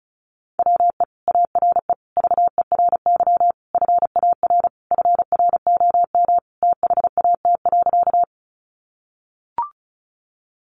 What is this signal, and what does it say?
Morse code, heard as a keyed tone, about 34 wpm, 700 Hz. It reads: WE ARE VERY FAR FROM THAT.
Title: Sentences from Top 300 Words 35wpm